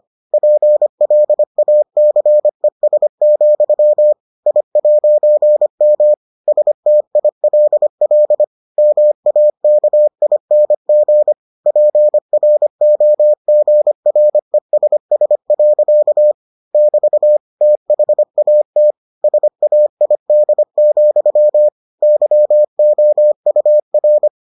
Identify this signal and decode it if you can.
Morse code, heard as a keyed tone, about 25 wpm, 600 Hz. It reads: PLACES, I'M STILL MAKING PROGRESS. <BT> THAT SAID, YOUR